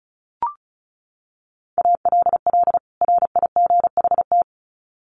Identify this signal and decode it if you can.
Morse code, heard as a keyed tone, about 34 wpm, 700 Hz. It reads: ALL RIGHT
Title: Sets of 2 Words 35wpm